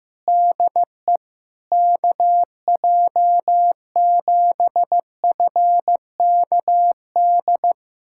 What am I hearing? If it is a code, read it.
Morse code, heard as a keyed tone, about 15 wpm, 700 Hz. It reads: DE KJ7FKD
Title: USA Callsign DE Callsign 15wpm